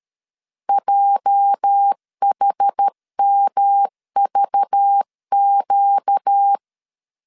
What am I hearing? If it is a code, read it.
Morse code, heard as a keyed tone, about 13 wpm, 800 Hz. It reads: JHMVQ
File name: LEZIONI / 9 28 12wpm 2351